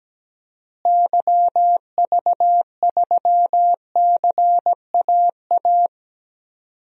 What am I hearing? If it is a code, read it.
Morse code, heard as a keyed tone, about 17 wpm, 700 Hz. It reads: YV3CAA
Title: DXCC-100 Call Signs - 6 Characters - Repeated 2x - 17wpm